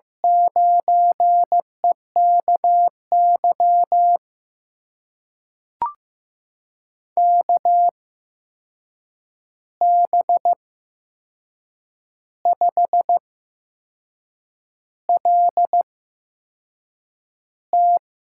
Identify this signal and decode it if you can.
Morse code, heard as a keyed tone, about 15 wpm, spaced as for 6 wpm, 700 Hz. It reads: 9EKY KB5LT
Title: USA Call Signs - 6 Characters - 8x Character Spacing - 15wpm